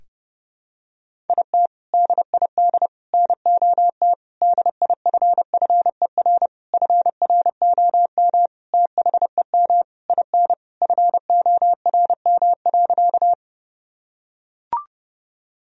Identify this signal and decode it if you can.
Morse code, heard as a keyed tone, about 30 wpm, 700 Hz. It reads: IT DID NOT DIFFER FROM THEM IN FORM.
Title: Sentences from Top 1000 Words - Review Part 1 - 30wpm